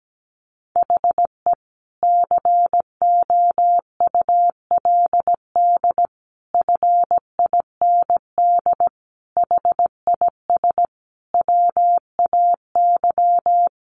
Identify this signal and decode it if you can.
Morse code, heard as a keyed tone, about 17 wpm, 700 Hz. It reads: HE COULD FIND HIS WAY
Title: Sentences from Top 1400 Words 17wpm